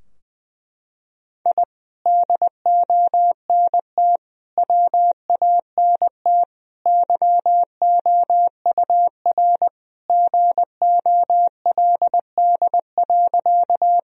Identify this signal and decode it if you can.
Morse code, heard as a keyed tone, about 20 wpm, 700 Hz. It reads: I DONT WANT YOUR GOLD.
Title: Sentences from Top 1000 Words - Review Part 3 - 20wpm